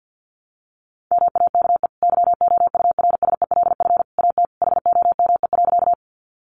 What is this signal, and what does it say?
Morse code, heard as a keyed tone, about 48 wpm, 700 Hz. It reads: MAKE YOURSELF AT HOME.
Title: Sentences from Top 600 Words 50wpm